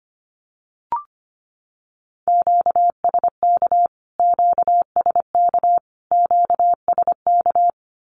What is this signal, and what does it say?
Morse code, heard as a keyed tone, about 25 wpm, 700 Hz. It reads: QSK QSK QSK
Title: Q-Codes Repeated 3x 25wpm